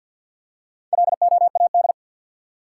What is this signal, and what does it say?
Morse code, heard as a keyed tone, about 53 wpm, 700 Hz. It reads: LOAD
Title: Four Letter Words - 50wpm